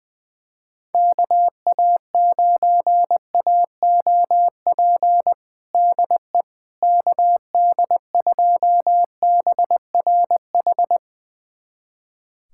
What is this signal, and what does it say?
Morse code, heard as a keyed tone, about 20 wpm, 700 Hz. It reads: KA9AOP DE KD2BRH
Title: USA Callsign DE Callsign - 6 Characters - 20wpm